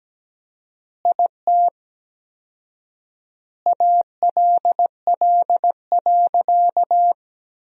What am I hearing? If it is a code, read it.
Morse code, heard as a keyed tone, about 17 wpm, 700 Hz. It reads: IT ALL.
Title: 4x Word-Spacing - Sentences from Top 200 Words - 17wpm